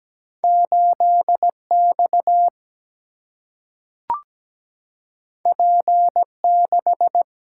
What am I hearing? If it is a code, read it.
Morse code, heard as a keyed tone, about 17 wpm, 700 Hz. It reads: 8X P6
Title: Single-Letter Single-Number 17wpm